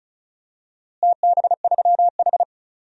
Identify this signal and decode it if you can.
Morse code, heard as a keyed tone, about 36 wpm, 700 Hz. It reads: TB3H
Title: DXCC-100 Call Signs - 4 Characters - Repeated 3x - 35wpm